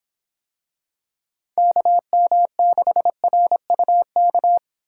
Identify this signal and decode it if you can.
Morse code, heard as a keyed tone, about 25 wpm, 700 Hz. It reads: KM6RUK